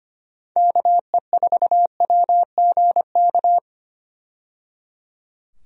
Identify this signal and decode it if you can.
Morse code, heard as a keyed tone, about 25 wpm, 700 Hz. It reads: KE4WGK
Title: USA Call Signs - 6 Characters - 25wpm